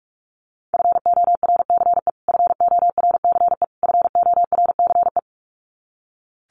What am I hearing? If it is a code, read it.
Morse code, heard as a keyed tone, about 44 wpm, 700 Hz. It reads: FORCE FORCE FORCE
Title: Top 500 Words Repeated 45wpm